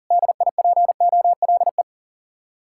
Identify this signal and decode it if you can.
Morse code, heard as a keyed tone, about 50 wpm, 700 Hz. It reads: DIPOLE
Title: Speed-Racing Common QSO Phrases - 40wpm